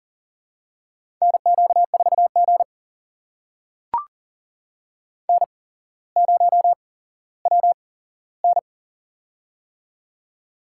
Morse code, heard as a keyed tone, about 39 wpm, spaced as for 17 wpm, 700 Hz. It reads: NQ4G N0WN
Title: USA Call Signs - 4 Characters - 8x Character Spacing - 40wpm